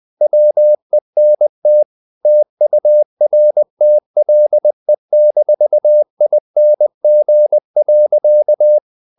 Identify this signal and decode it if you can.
Morse code, heard as a keyed tone, about 20 wpm, 600 Hz. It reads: WENT TURTLE-ING.
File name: the old man and the sea / the old man and the sea part 1 20wpm